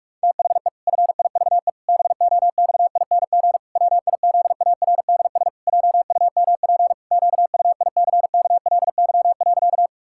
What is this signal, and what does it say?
Morse code, heard as a keyed tone, about 46 wpm, 700 Hz. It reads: THE FIVE BOXING WIZARDS JUMP QUICKLY.
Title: Pangram Challenge v2 - 45wpm